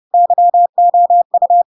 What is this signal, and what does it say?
Morse code, heard as a keyed tone, about 30 wpm, 700 Hz. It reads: YOU
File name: Pinocchio / Pinocchio ch1 30 30 ik1hge